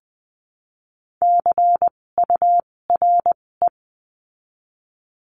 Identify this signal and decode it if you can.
Morse code, heard as a keyed tone, about 20 wpm, spaced as for 16 wpm, 700 Hz. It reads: CURE
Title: 256 Lesson 42a - Words 12-20wpm